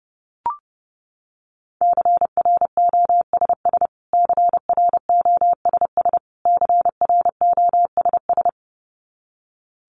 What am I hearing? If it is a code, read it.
Morse code, heard as a keyed tone, about 30 wpm, 700 Hz. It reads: CROSS CROSS CROSS
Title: Top 300 Words Repeated 30wpm